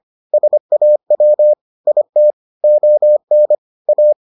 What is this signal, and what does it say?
Morse code, heard as a keyed tone, about 25 wpm, 600 Hz. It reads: SAW IT ON A